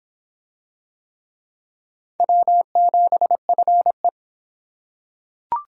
Morse code, heard as a keyed tone, about 24 wpm, 700 Hz. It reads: W7FE